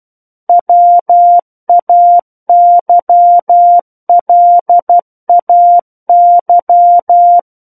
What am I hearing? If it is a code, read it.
Morse code, heard as a keyed tone, about 12 wpm, 700 Hz. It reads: WAYLAY
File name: Book One / 03-Chapter 3 - 12 WPM